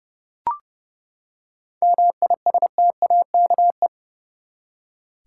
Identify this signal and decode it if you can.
Morse code, heard as a keyed tone, about 30 wpm, 700 Hz. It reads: MISTAKE
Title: Seven Letter Words - 30wpm